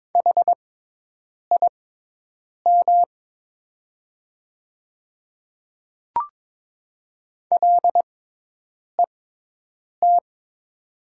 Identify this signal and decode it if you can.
Morse code, heard as a keyed tone, about 22 wpm, spaced as for 8 wpm, 700 Hz. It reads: HIM LET
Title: 6x Character Spacing - Top 300 Words - 22wpm